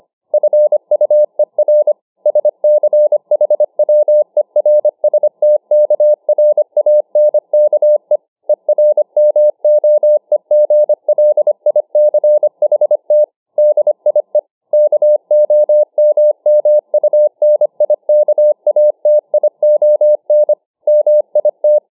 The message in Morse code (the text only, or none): FUER SCHWERSTKRANKE ERMOEGLICHT DIE KOMMUNIKATION MIT